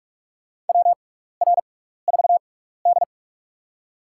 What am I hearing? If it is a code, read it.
Morse code, heard as a keyed tone, about 47 wpm, spaced as for 16 wpm, 700 Hz. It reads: WR4D
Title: USA Call Signs - 6x Character Spacing - 45wpm